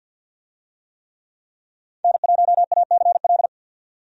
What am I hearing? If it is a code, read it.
Morse code, heard as a keyed tone, about 47 wpm, 700 Hz. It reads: N1ACL
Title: USA Call Signs - 5 Characters - 50wpm